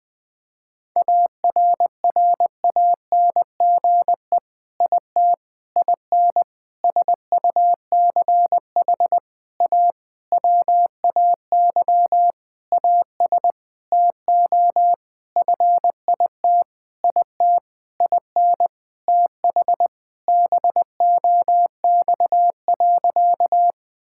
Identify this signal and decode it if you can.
Morse code, heard as a keyed tone, about 20 wpm, 700 Hz. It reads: ARRANGE IT IN SUCH A WAY AS TO FIT IT IN TH BOX.
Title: Sentences with 6-Word Phrases - 20wpm